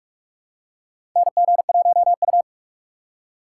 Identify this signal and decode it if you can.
Morse code, heard as a keyed tone, about 44 wpm, 700 Hz. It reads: NG1U
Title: USA Call Signs - 4 Characters - 2x Character Spacing - 45wpm